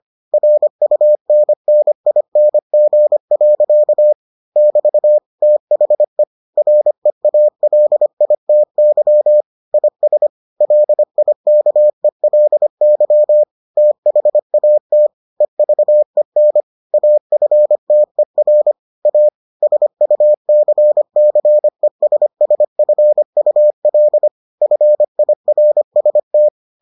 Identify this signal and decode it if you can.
Morse code, heard as a keyed tone, about 25 wpm, 600 Hz. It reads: RUNNING. <BT> THE REALITY IS LIKELY THAT EVEN AFTER A SUCCESSFUL FIRST